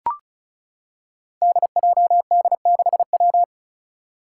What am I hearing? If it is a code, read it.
Morse code, heard as a keyed tone, about 36 wpm, 700 Hz. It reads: DJD6W
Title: Five Challenging Letter-Number Combinations 35wpm